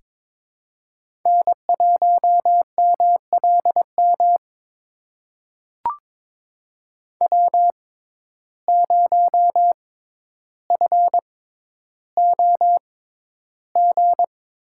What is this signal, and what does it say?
Morse code, heard as a keyed tone, about 22 wpm, spaced as for 11 wpm, 700 Hz. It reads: N1MLM W0FOG
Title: USA Call Signs - 5 Characters - 6x Character Spacing - 22wpm